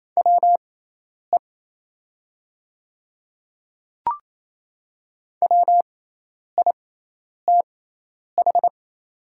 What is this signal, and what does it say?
Morse code, heard as a keyed tone, about 28 wpm, spaced as for 10 wpm, 700 Hz. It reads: WE WITH
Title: 6x Character Spacing - Top 100 Words - 28wpm